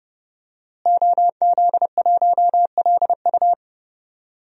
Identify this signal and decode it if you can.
Morse code, heard as a keyed tone, about 30 wpm, 700 Hz. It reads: OZ1LU
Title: DXCC-100 Call Signs - 5 Characters - Repeated 2x - 30wpm